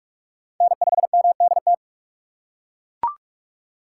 Morse code, heard as a keyed tone, about 42 wpm, 700 Hz. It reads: N5MDT
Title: USA Call Signs - 5 Characters - 2x Character Spacing - 45wpm